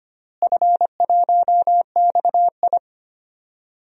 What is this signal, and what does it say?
Morse code, heard as a keyed tone, about 25 wpm, 700 Hz. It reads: F1XI
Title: DXCC-100 Call Signs - 4 Characters - Repeated 3x - 25wpm